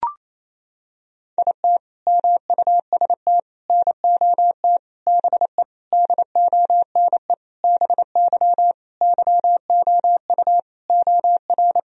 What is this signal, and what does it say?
Morse code, heard as a keyed tone, about 28 wpm, 700 Hz. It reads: IT MUST NOT BE DONE BY YOU OR
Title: Sentences from Top 1000 Words - Review Part 1 - 28wpm